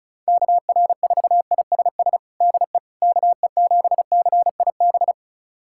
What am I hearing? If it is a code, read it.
Morse code, heard as a keyed tone, about 36 wpm, 700 Hz. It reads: KR4ISS DE KE7CIB
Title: USA Callsign DE Callsign - 6 Characters - 35wpm